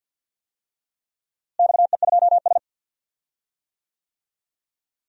Morse code, heard as a keyed tone, about 46 wpm, 700 Hz. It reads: XE2S
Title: DXCC-100 Call Signs - 4 Characters - Repeated 3x - 50wpm